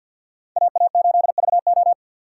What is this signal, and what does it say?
Morse code, heard as a keyed tone, about 53 wpm, 700 Hz. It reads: AA8VO
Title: USA Call Signs - 5 Characters - 6x Character Spacing - 50wpm